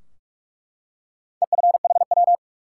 Morse code, heard as a keyed tone, about 47 wpm, 700 Hz. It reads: EFHW
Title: Top 180 QSO Elements 45wpm